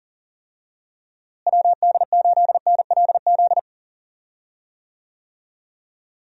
Morse code, heard as a keyed tone, about 40 wpm, 700 Hz. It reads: WD8NLZ
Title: USA Call Signs - 6 Characters - Repeated 2x - 40wpm